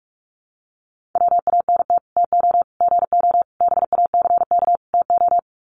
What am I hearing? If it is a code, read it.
Morse code, heard as a keyed tone, about 44 wpm, 700 Hz. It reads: WANT TO GO BACK TO
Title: Sets of 5 Words 45wpm